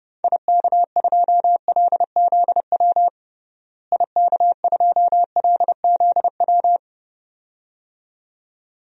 Morse code, heard as a keyed tone, about 30 wpm, 700 Hz. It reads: IK2LZW IK2LZW
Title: DXCC-100 Call Signs - 6 Characters - Repeated 2x - 30wpm